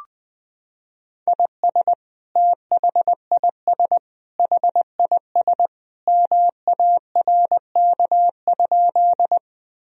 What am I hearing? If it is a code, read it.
Morse code, heard as a keyed tone, about 20 wpm, 700 Hz. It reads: IS THIS HIS MARK?
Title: Sentences from Top 1000 Words - Review Part 1 - 20wpm